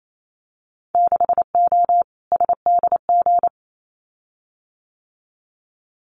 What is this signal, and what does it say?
Morse code, heard as a keyed tone, about 28 wpm, 700 Hz. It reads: 6O SDG
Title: Winter Field Day Exchanges Encore 28wpm